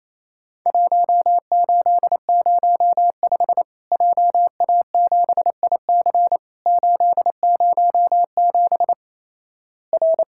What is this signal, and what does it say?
Morse code, heard as a keyed tone, about 28 wpm, 700 Hz. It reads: 1805 JA7IC 807 R
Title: Worked All Europe DX Contest - QTC Exchange - 28wpm